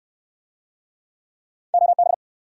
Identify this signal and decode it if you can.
Morse code, heard as a keyed tone, about 66 wpm, 700 Hz. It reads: OB